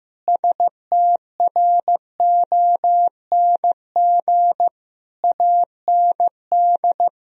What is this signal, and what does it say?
Morse code, heard as a keyed tone, about 16 wpm, 700 Hz. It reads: STRONG AND
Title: Sentences from Top 1000 Words - Review Part 3 - 15wpm